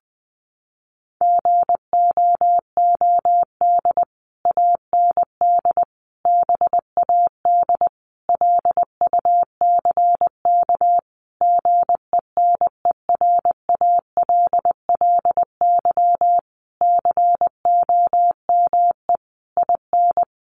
Morse code, heard as a keyed tone, about 20 wpm, 700 Hz. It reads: GOOD AND BAD LUCK GENERALLY COME IN
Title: Sentences from Top 1500 Words 20wpm